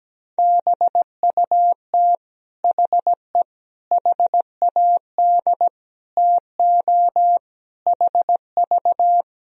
Morse code, heard as a keyed tone, about 17 wpm, 700 Hz. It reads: BUT HE HAD TO HV
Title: Sentences from Top 100 Words and Abbreviations 17wpm